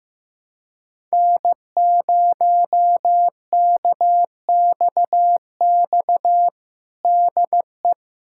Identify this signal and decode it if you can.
Morse code, heard as a keyed tone, about 15 wpm, 700 Hz. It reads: N0KXX DE
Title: USA Callsign DE Callsign - 5 Characters - 15wpm